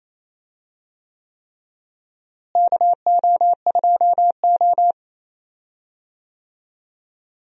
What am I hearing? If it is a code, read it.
Morse code, heard as a keyed tone, about 28 wpm, 700 Hz. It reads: KO2O
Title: USA Call Signs - 4 Characters - Repeated 3x - 28wpm